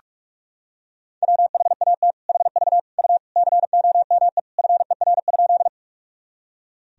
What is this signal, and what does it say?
Morse code, heard as a keyed tone, about 41 wpm, 700 Hz. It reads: WHAT HV U COME FER?
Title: Sentences from Top 100 Words and Abbreviations 45wpm